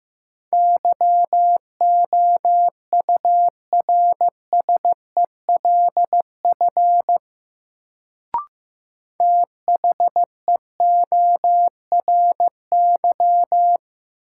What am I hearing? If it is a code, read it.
Morse code, heard as a keyed tone, about 15 wpm, 700 Hz. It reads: YOURSELF THEORY
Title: Sentences from Top 600 Words - New Word Drill - 15wpm